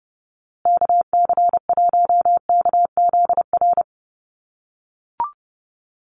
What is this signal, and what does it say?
Morse code, heard as a keyed tone, about 30 wpm, 700 Hz. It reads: KC1KZR
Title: USA Call Signs Repeated 2x 30wpm